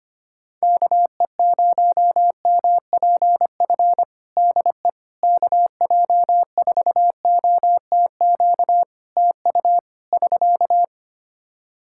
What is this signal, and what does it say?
Morse code, heard as a keyed tone, about 25 wpm, 700 Hz. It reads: KE0MPF DE KJ4OTQ TU <SK>
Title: Practice QSO - Exchange 4 - Ending - 25wpm